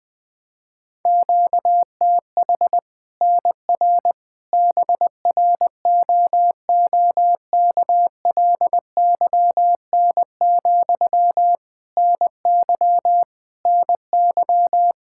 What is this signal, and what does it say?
Morse code, heard as a keyed tone, about 21 wpm, 700 Hz. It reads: QTH NR BROOKLYN, NY NY
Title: Practice QSO - Exchange 1 - Beginning - 20wpm